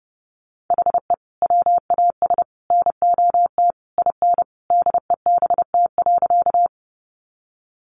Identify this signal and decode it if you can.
Morse code, heard as a keyed tone, about 30 wpm, 700 Hz. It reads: HE WAS NOT IN DEBT.
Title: Sentences from Top 1300 Words 30wpm